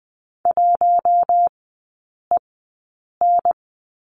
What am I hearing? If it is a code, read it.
Morse code, heard as a keyed tone, about 20 wpm, spaced as for 8 wpm, 700 Hz. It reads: 1EN
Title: 074 Lesson 13e - Letter L - Three Characters 20wpm